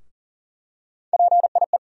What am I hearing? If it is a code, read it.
Morse code, heard as a keyed tone, about 40 wpm, 700 Hz. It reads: PIE
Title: USA Call Sign Postfixes - 3 Letters 40wpm